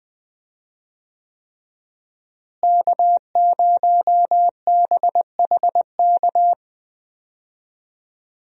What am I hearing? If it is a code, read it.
Morse code, heard as a keyed tone, about 20 wpm, 700 Hz. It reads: K0BHK
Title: USA Call Signs - 5 Characters - Repeated 3x - 20wpm